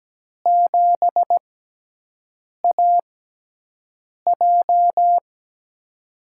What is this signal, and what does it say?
Morse code, heard as a keyed tone, about 17 wpm, spaced as for 6 wpm, 700 Hz. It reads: 7AJ
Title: USA Call Signs - 4 Characters - 6x Character Spacing - 17wpm